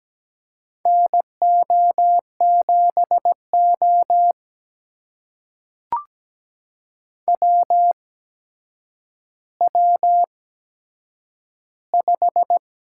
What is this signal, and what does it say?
Morse code, heard as a keyed tone, about 17 wpm, spaced as for 8 wpm, 700 Hz. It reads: NO7O WW5
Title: USA Call Signs - 4 Characters - 8x Character Spacing - 17wpm